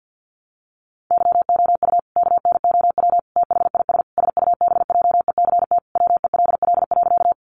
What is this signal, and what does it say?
Morse code, heard as a keyed tone, about 50 wpm, 700 Hz. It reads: YOU KNOW THIS SUBJECT WELL.
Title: Sentences from Top 800 Words 50wpm